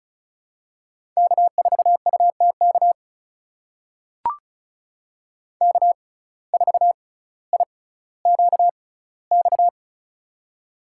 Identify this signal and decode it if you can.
Morse code, heard as a keyed tone, about 36 wpm, spaced as for 18 wpm, 700 Hz. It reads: K4UTK K4IQX
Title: USA Call Signs - 5 Characters - 6x Character Spacing - 35wpm